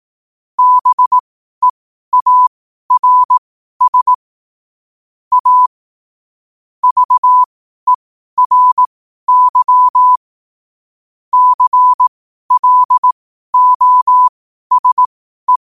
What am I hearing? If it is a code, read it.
Morse code, heard as a keyed tone, about 18 wpm, spaced as for 12 wpm, 1000 Hz. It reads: BEARS A VERY CLOSE
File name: qotdfiles / 12wpm